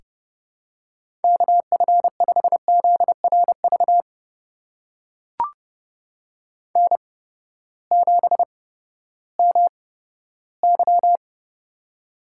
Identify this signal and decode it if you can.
Morse code, heard as a keyed tone, about 30 wpm, spaced as for 15 wpm, 700 Hz. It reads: KF5ZRV N7MY
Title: USA Call Signs - 8x Character Spacing - 30wpm